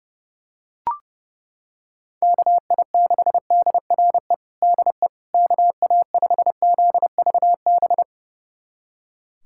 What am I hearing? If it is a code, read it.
Morse code, heard as a keyed tone, about 30 wpm, 700 Hz. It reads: KI6DRE DE KA5ZVB
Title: USA Callsign DE Callsign 30wpm